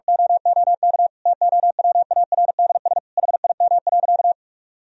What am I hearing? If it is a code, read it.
Morse code, heard as a keyed tone, about 46 wpm, 700 Hz. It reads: OOK TOWARDS HIM.
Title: Sentences from Top 1000 Words - Review Part 2 - 45wpm